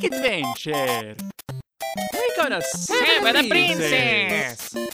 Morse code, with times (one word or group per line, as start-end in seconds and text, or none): none